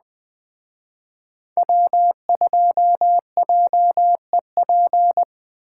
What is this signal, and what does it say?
Morse code, heard as a keyed tone, about 20 wpm, 700 Hz. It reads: W2JEP
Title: Speed Racing - USA 1x3 Call Signs 17wpm